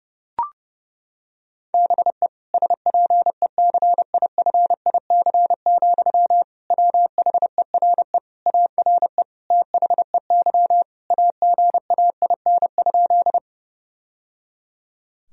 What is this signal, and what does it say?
Morse code, heard as a keyed tone, about 32 wpm, 700 Hz. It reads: BE SPECIFIC, WHERE ARE THEY AGAIN?
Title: Sentences from Top 1000 Words - Review Part 3 - 30wpm